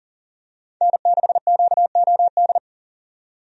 Morse code, heard as a keyed tone, about 40 wpm, 700 Hz. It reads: N6QOD
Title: USA Call Signs - 5 Characters - 40wpm